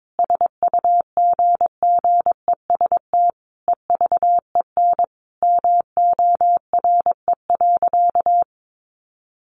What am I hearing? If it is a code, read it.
Morse code, heard as a keyed tone, about 22 wpm, 700 Hz. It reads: SUGGEST EVEN MORE.
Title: Sentences from Top 1300 Words 22wpm